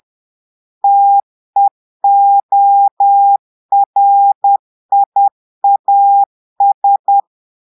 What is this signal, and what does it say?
Morse code, heard as a keyed tone, about 10 wpm, 800 Hz. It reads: TEORIAS
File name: Portugais - texte en clair 1 - vitesse - 10 mots minute